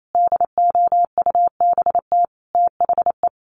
Morse code, heard as a keyed tone, about 28 wpm, 700 Hz. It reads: DOUBT THE
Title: Sentences from Top 1500 Words 28wpm